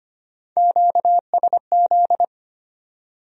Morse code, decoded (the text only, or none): QSZ